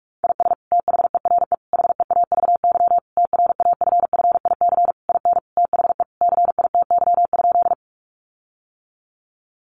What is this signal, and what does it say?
Morse code, heard as a keyed tone, about 44 wpm, 700 Hz. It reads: IS THERE HEAVY TRAFFIC IN THE CITY?
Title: Sentences from Top 1300 Words 45wpm